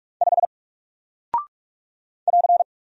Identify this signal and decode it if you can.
Morse code, heard as a keyed tone, about 44 wpm, 700 Hz. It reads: 5 <AR>
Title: Single Character (Letter:Number:Punctuation:Prosign) Alt 45wpm